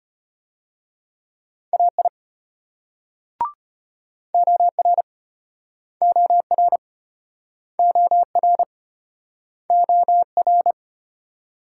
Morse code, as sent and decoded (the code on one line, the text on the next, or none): .- .. / --- .-. / --- .-. / --- .-. / --- .-.
AI OR OR OR OR